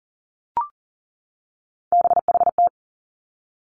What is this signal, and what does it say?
Morse code, heard as a keyed tone, about 41 wpm, 700 Hz. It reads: BHT
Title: Three Challenging Letter-Number Combinations 40wpm